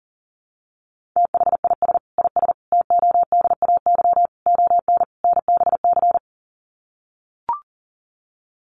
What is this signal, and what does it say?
Morse code, heard as a keyed tone, about 40 wpm, 700 Hz. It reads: THIS IS TODAY ON NBC
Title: Sets of 5 Words Encore 40wpm